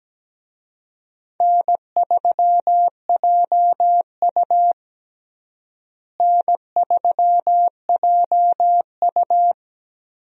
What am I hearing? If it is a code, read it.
Morse code, heard as a keyed tone, about 17 wpm, 700 Hz. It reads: N3JU N3JU
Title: USA Call Signs - 4 Characters - Repeated 3x - 17wpm